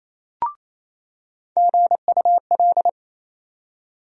Morse code, heard as a keyed tone, about 28 wpm, 700 Hz. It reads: GUL
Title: USA Call Sign Postfixes - 3 Letters 28wpm